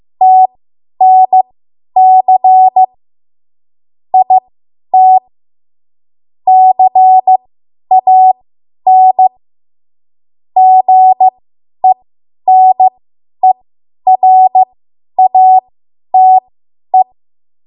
Morse code, heard as a keyed tone, about 15 wpm, spaced as for 10 wpm, 750 Hz. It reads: TNC IT CAN GENERATE
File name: Morse / 10WPM